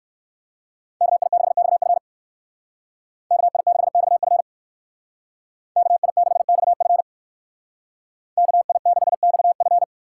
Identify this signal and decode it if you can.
Morse code, heard as a keyed tone, about 59 wpm, 700 Hz. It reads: KI6XF KI6XF KI6XF KI6XF